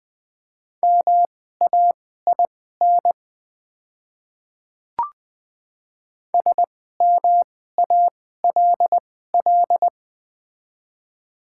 Morse code, heard as a keyed tone, about 20 wpm, spaced as for 14 wpm, 700 Hz. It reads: MAIN SMALL
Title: 2x Character Spacing - Top 500 Words - 20wpm